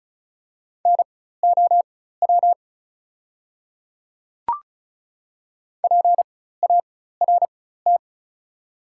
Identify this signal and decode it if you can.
Morse code, heard as a keyed tone, about 36 wpm, spaced as for 16 wpm, 700 Hz. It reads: NOW PART
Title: 4x Character Spacing - Top 300 Words - 35wpm